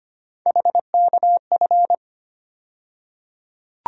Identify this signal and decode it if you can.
Morse code, heard as a keyed tone, about 25 wpm, 700 Hz. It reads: HKF
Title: USA Call Sign Postfixes - 3 Letters 25wpm